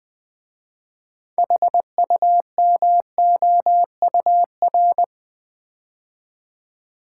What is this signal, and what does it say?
Morse code, heard as a keyed tone, about 20 wpm, 700 Hz. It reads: HUMOUR
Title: 6 Letter Words - First 3 Letters Given - 20wpm